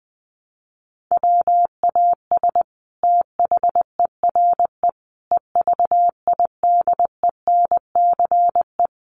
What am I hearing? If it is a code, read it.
Morse code, heard as a keyed tone, about 20 wpm, 700 Hz. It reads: WAS THERE EVIDENCE
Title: Sentences from Top 1300 Words 20wpm